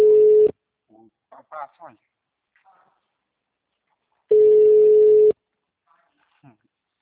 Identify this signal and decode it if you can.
Ringback tone (Europe, 425 Hz), with speech.
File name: out- -2038- - - .340